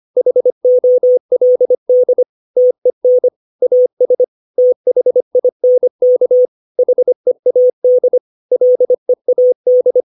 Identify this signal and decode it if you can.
Morse code, heard as a keyed tone, about 25 wpm, 500 Hz. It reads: HOLD TEN AS THINK HEAD LEAD
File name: 632 words / 632 Words 25WPM